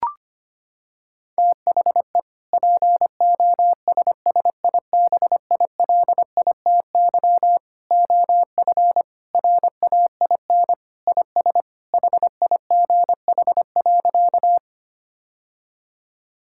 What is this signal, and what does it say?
Morse code, heard as a keyed tone, about 25 wpm, 700 Hz. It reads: THE POSSIBILITY OF RAIN IS HIGH.
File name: practice / 25wpm / Sentences with 3-Word Phrases - 25wpm